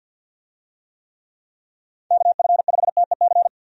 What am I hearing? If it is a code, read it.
Morse code, heard as a keyed tone, about 51 wpm, 700 Hz. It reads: KF5TEC
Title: USA Call Signs - 6 Characters - 50wpm